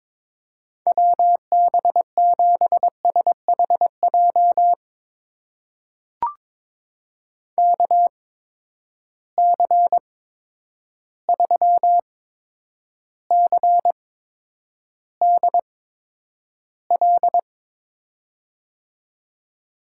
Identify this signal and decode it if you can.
Morse code, heard as a keyed tone, about 22 wpm, spaced as for 9 wpm, 700 Hz. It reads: WB7SHJ KC3CDL
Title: USA Call Signs - 6 Characters - 8x Character Spacing - 22wpm